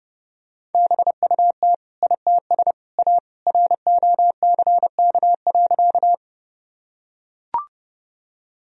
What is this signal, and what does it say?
Morse code, heard as a keyed tone, about 30 wpm, 700 Hz. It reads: BUT ITS A ROCK.
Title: Sentences from Top 1000 Words - Review Part 2 - 30wpm